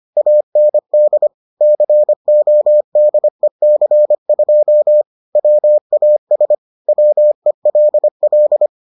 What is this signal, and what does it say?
Morse code, heard as a keyed tone, about 25 wpm, 600 Hz. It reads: AND CODEC2 WAS WELL